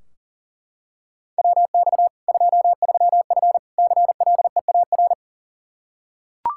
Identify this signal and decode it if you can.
Morse code, heard as a keyed tone, about 37 wpm, 700 Hz. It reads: WX 23F CLEAR
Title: K1USN SST QSO Practice 40wpm